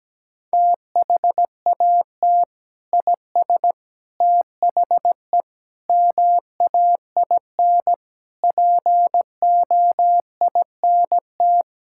Morse code, heard as a keyed tone, about 17 wpm, 700 Hz. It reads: THAT IS THE MAIN POINT
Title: Sentences from Top 1000 Words - Review Part 5 - 17wpm